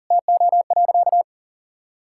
Morse code, heard as a keyed tone, about 40 wpm, 700 Hz. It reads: TO.
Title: Sentences from Top 1000 Words - Review Part 5 - 40wpm